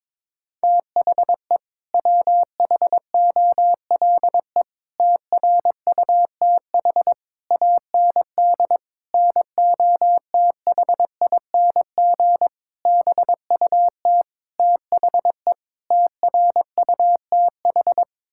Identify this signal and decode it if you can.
Morse code, heard as a keyed tone, about 21 wpm, 700 Hz. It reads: THE WHOLE TRUTH AND NOTHING BUT THE TRUTH
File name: practice / 22wpm / Sets of 8 Words - 22wpm